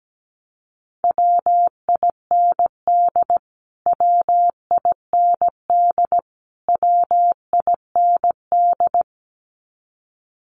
Morse code, heard as a keyed tone, about 17 wpm, 700 Hz. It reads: WIND WIND WIND
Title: Top 500 Words Repeated 17wpm